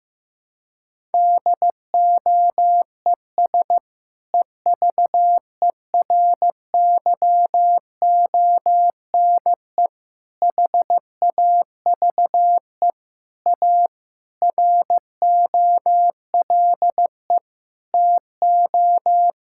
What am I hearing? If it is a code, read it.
Morse code, heard as a keyed tone, about 15 wpm, 700 Hz. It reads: DOES EVERYONE HAVE A ROLE TO
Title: Sentences from Top 1000 Words - Review Part 6 - 15wpm